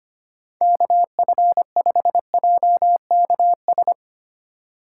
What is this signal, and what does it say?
Morse code, heard as a keyed tone, about 25 wpm, 700 Hz. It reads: KF5JKS
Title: USA Call Signs - 6 Characters - 3x Character Spacing - 25wpm